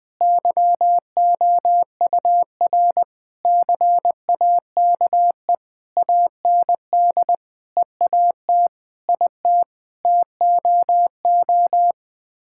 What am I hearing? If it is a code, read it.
Morse code, heard as a keyed tone, about 21 wpm, 700 Hz. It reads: YOUR CAKE AND EAT IT TOO
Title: Sets of 9 Words - 20wpm